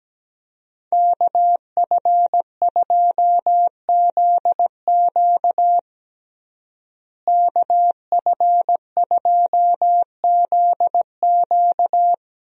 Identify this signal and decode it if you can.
Morse code, heard as a keyed tone, about 17 wpm, 700 Hz. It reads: KF2ZQ KF2ZQ
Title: USA Call Signs - 5 Characters - Repeated 3x - 17wpm